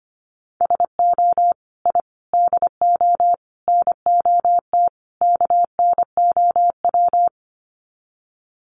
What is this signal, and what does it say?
Morse code, heard as a keyed tone, about 25 wpm, 700 Hz. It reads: SO I DO NOT KNOW
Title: Sets of 5 Words Encore 25wpm